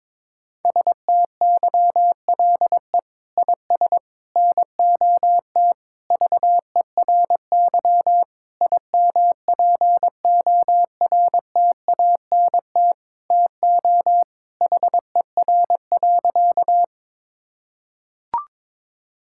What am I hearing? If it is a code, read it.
Morse code, heard as a keyed tone, about 22 wpm, 700 Hz. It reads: STYLE IS NOT VERY IMPORTANT TO HER.
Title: Sentences from Top 1000 Words - Review Part 5 - 22wpm